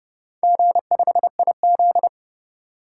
Morse code, heard as a keyed tone, about 28 wpm, 700 Hz. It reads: G5IZ